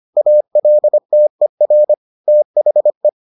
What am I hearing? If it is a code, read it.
Morse code, heard as a keyed tone, about 25 wpm, 600 Hz. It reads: ALTER THE